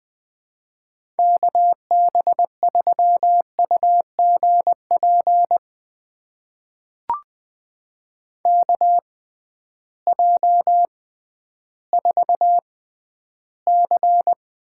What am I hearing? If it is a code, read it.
Morse code, heard as a keyed tone, about 20 wpm, spaced as for 12 wpm, 700 Hz. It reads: KB3UGP KJ4C
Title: USA Call Signs - 6 Characters - 6x Character Spacing - 20wpm